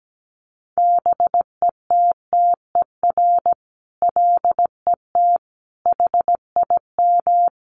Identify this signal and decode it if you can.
Morse code, heard as a keyed tone, about 17 wpm, 700 Hz. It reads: BETTER LET HIM
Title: Sentences from Top 1200 Words 17wpm